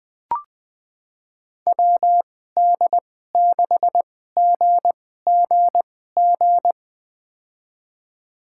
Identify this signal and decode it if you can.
Morse code, heard as a keyed tone, about 20 wpm, spaced as for 15 wpm, 700 Hz. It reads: WD6GGG
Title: USA Call Signs - 6 Characters - 2x Character Spacing - 20wpm